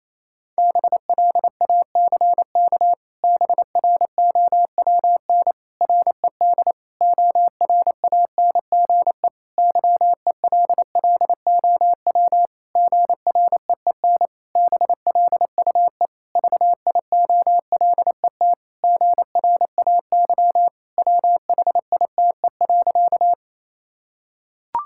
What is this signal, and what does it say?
Morse code, heard as a keyed tone, about 28 wpm, 700 Hz. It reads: BLACK BROWN RED ORANGE YELLOW GREEN BLUE VIOLET GRAY WHITE.
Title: Pangram Challenge v2 - 28wpm